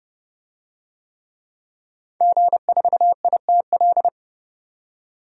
Morse code, heard as a keyed tone, about 30 wpm, 700 Hz. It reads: G4ITL
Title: DXCC-100 Call Signs - 5 Characters - 30wpm